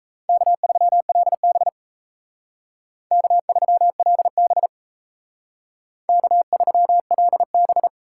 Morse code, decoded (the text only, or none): K3LB K3LB K3LB